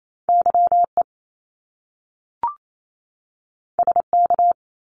Morse code, heard as a keyed tone, about 28 wpm, 700 Hz. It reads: YE SK
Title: Top Two Letter Combinations in Words 28wpm